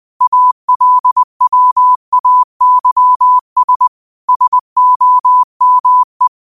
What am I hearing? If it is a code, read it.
Morse code, heard as a keyed tone, about 20 wpm, 1000 Hz. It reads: ALWAYS SOME
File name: qotdfiles / 20wpm